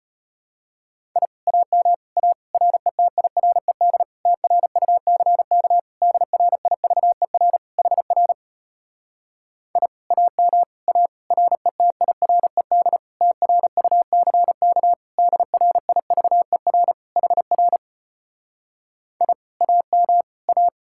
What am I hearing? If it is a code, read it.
Morse code, heard as a keyed tone, about 33 wpm, 700 Hz. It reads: I AM A RETIRED TRUCK DRIVER HR I AM A RETIRED TRUCK DRIVER HR I AM A